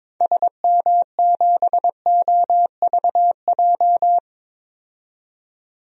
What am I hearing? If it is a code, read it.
Morse code, heard as a keyed tone, about 22 wpm, 700 Hz. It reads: SM7OVJ
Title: DXCC-100 Call Signs - 6 Characters - 22wpm